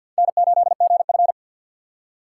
Morse code, heard as a keyed tone, about 60 wpm, 700 Hz. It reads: N8GF
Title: Speed Racing - USA 1x2 Call Signs 50wpm